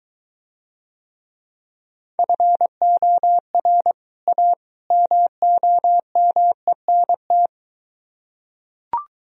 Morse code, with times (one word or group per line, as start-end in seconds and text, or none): 2.2-3.9 FOR
4.3-4.5 A
4.9-7.5 MOMENT